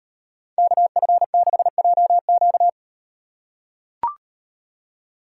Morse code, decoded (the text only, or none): KF6JQ